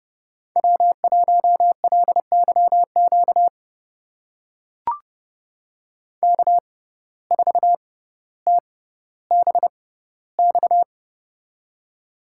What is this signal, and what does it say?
Morse code, heard as a keyed tone, about 30 wpm, spaced as for 15 wpm, 700 Hz. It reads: W1LYQ K4TBX
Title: USA Call Signs - 5 Characters - 6x Character Spacing - 30wpm